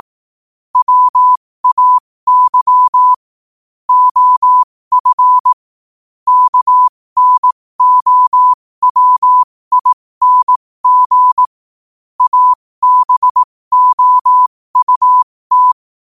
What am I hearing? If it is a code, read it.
Morse code, heard as a keyed tone, about 18 wpm, spaced as for 15 wpm, 1000 Hz. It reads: WAY OF KNOWING ABOUT
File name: qotdfiles / 15wpm